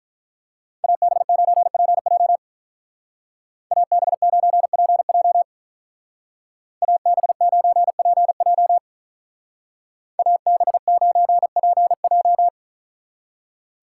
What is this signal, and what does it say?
Morse code, heard as a keyed tone, about 44 wpm, 700 Hz. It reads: AB9PJ AB9PJ AB9PJ AB9PJ